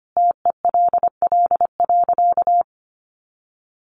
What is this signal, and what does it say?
Morse code, heard as a keyed tone, about 25 wpm, 700 Hz. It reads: TELL.
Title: Sentences from Top 1200 Words 25wpm